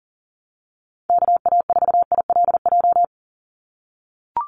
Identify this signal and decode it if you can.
Morse code, heard as a keyed tone, about 39 wpm, 700 Hz. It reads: KA4ILJ
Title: USA Call Signs 40wpm